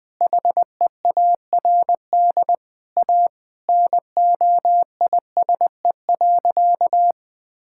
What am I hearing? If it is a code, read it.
Morse code, heard as a keyed tone, about 20 wpm, 700 Hz. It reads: HEARD A NOISE.
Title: Sentences with 5-Word Phrases - 20wpm